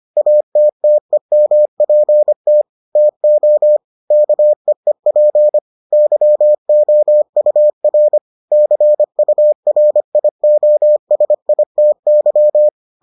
Morse code, heard as a keyed tone, about 25 wpm, 600 Hz. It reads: ATTEMPT TO KEEP YOUR CURIOSITY